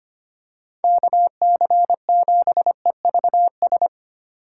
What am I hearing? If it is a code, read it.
Morse code, heard as a keyed tone, about 25 wpm, 700 Hz. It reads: KC7EVS
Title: USA Call Signs - 6 Characters - 4x Character Spacing - 25wpm